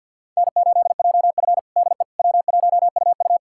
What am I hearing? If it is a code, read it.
Morse code, heard as a keyed tone, about 53 wpm, 700 Hz. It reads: N8JF DE W1UU
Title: USA Callsign DE Callsign - 4 Characters - 50wpm